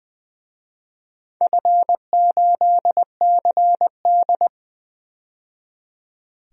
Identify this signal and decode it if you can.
Morse code, heard as a keyed tone, about 20 wpm, 700 Hz. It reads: F8CD
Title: DXCC-100 Call Signs - 4 Characters - 20wpm